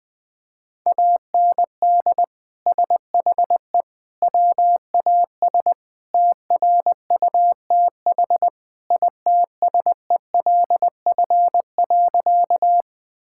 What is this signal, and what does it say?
Morse code, heard as a keyed tone, about 20 wpm, 700 Hz. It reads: AND SHE WAS TRUTH ITSELF.
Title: Sentences from Top 1000 Words - Review Part 5 - 20wpm